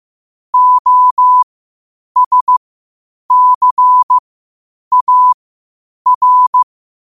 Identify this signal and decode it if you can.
Morse code, heard as a keyed tone, about 15 wpm, spaced as for 8 wpm, 1000 Hz. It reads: OSCAR
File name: qotdfiles / 8wpm